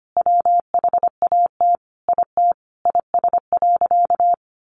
Morse code, heard as a keyed tone, about 25 wpm, 700 Hz. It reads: WHAT IT IS.
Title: Sentences from Top 700 Words 25wpm